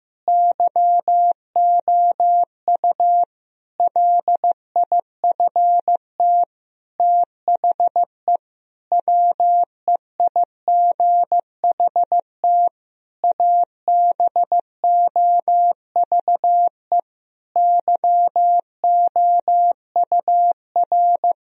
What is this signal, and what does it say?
Morse code, heard as a keyed tone, about 15 wpm, 700 Hz. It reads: YOU LIFT THE WEIGHT ABOVE YOUR
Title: Sentences from Top 1000 Words - Review Part 3 - 15wpm